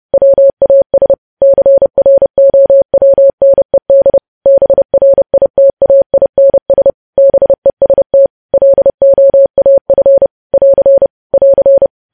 Morse code, heard as a keyed tone, about 30 wpm, 550 Hz. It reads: WAS CROWNED BRITAINS BEST LOAF <AR> <AR>
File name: podcast / News headlines-1x-30x30